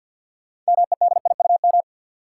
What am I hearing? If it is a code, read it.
Morse code, heard as a keyed tone, about 53 wpm, 700 Hz. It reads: MEDIUM
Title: Sentences from Top 900 Words - New Word Drill - 50wpm